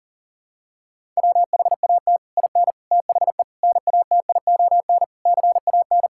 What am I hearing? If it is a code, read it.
Morse code, heard as a keyed tone, about 40 wpm, 700 Hz. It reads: WHAT IN THE NATION CAN
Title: Sentences from Top 1000 Words - Review Part 6 - 40wpm